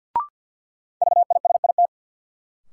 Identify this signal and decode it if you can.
Morse code, heard as a keyed tone, about 47 wpm, 700 Hz. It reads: VISIT
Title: Sentences from Top 900 Words - New Word Drill - 50wpm